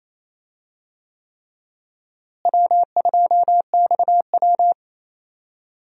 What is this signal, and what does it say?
Morse code, heard as a keyed tone, about 28 wpm, 700 Hz. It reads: W2XW
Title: USA Call Signs - 4 Characters - Repeated 3x - 28wpm